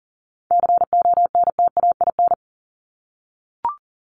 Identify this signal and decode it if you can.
Morse code, heard as a keyed tone, about 40 wpm, 700 Hz. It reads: CONTAIN
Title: ING Suffix - Top 100 Words with Base Words 40wpm